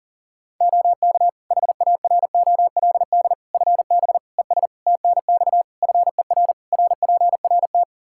Morse code, heard as a keyed tone, about 40 wpm, 700 Hz. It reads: OK HAROLD FB ES TNX FER RPRT
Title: Practice QSO - Exchange 2 - Info - 40wpm